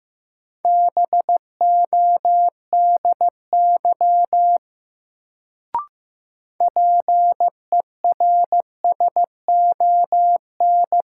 Morse code, heard as a keyed tone, about 15 wpm, 700 Hz. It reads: BODY PERSON
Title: Sentences from Top 400 Words - New Word Drill - 15wpm